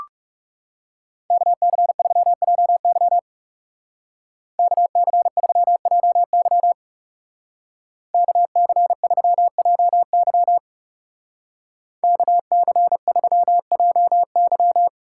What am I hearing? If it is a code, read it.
Morse code, heard as a keyed tone, about 36 wpm, 700 Hz. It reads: KC3JY KC3JY KC3JY KC3JY
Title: Speed Racing - USA 2x2 Call Signs 30wpm